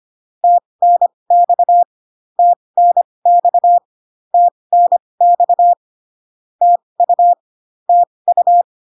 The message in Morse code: TNX TNX TNX TU TU

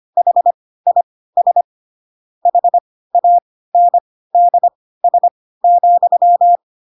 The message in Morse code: HIS HANDS,

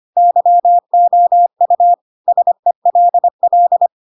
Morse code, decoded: YOU SELL

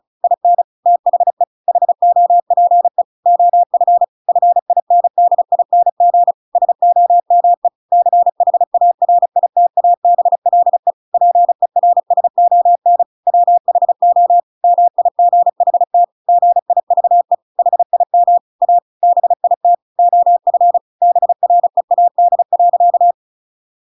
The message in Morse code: IN THE HOPE OF FINDING SOME CHARITABLE PERSON WHO MIGHT GIVE HIM A BIT OF BREAD.

